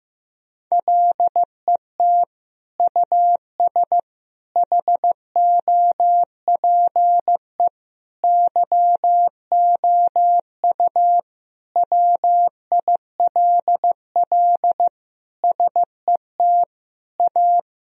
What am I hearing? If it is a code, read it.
Morse code, heard as a keyed tone, about 15 wpm, 700 Hz. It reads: LET US HOPE YOU WILL SET A
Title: Sentences from Top 1000 Words - Review Part 1 - 15wpm